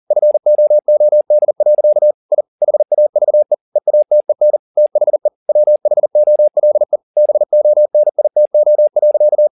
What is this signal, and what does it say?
Morse code, heard as a keyed tone, about 40 wpm, 600 Hz. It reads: FOOD. I HAVE EATEN THE WHOLE BONITO.
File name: the old man and the sea 2 40wpm